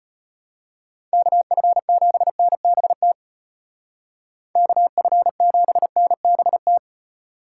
Morse code, heard as a keyed tone, about 34 wpm, 700 Hz. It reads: KF7NBT KF7NBT